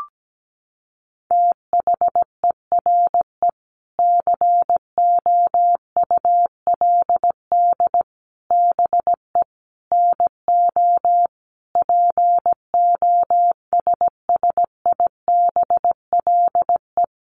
Morse code, heard as a keyed tone, about 17 wpm, 700 Hz. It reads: THERE COULD BE NO POSSIBLE
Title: Sentences from Top 700 Words 17wpm